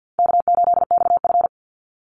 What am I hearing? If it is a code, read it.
Morse code, heard as a keyed tone, about 50 wpm, 700 Hz. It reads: K8XF
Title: Easy Call Signs Repeated 2x - 50wpm